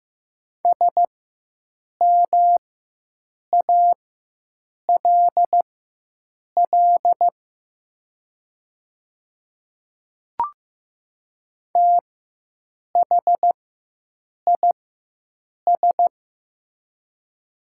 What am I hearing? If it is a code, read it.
Morse code, heard as a keyed tone, about 15 wpm, spaced as for 7 wpm, 700 Hz. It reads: SMALL THIS
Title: 4x Character Spacing - Top 200 Words - 15wpm